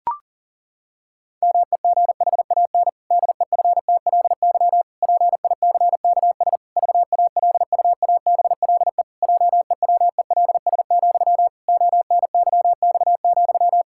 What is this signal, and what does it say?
Morse code, heard as a keyed tone, about 40 wpm, 700 Hz. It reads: MEGHAN DEFTLY PICKS VALUABLE JEWELS, ONYX,
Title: Pangram Challenge v2 - 40wpm